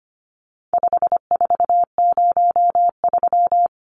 Morse code, heard as a keyed tone, about 25 wpm, 700 Hz. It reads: 5403
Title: Numbers - 4-Digit 25wpm